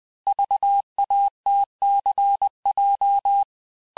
Morse code, heard as a keyed tone, about 20 wpm, 800 Hz. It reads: VATCJ
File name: cours-du-lundi files / cours-1584-teste-chiffre-vitesse-19mpm